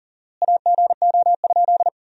Morse code, heard as a keyed tone, about 40 wpm, 700 Hz. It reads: AGO?
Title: 4x Word-Spacing - Sentences from Top 500 Words - 40wpm